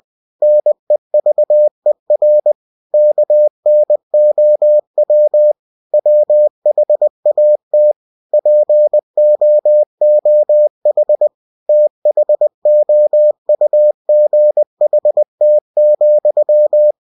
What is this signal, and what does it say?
Morse code, heard as a keyed tone, about 20 wpm, 600 Hz. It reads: NEVER KNOW WHAT POOH THOUGHT,
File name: cw-books / winnie-the-pooh / winnie-20wpm-page-0085